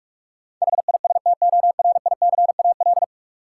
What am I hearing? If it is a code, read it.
Morse code, heard as a keyed tone, about 46 wpm, 700 Hz. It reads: HISTORICAL
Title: Sentences from Top 600 Words - New Word Drill - 45wpm